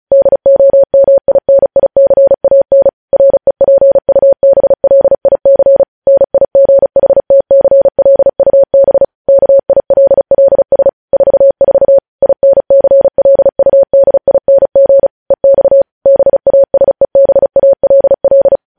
Morse code, heard as a keyed tone, about 36 wpm, 550 Hz. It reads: DOMINICAN REPUBLIC NIGHTCLUB KILLS 44 INCLUDING EX BASEBALL
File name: podcast / News headlines-1x-35x35